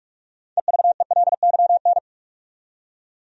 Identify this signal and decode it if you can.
Morse code, heard as a keyed tone, about 41 wpm, 700 Hz. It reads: EVELYN
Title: USA Common Names - 45wpm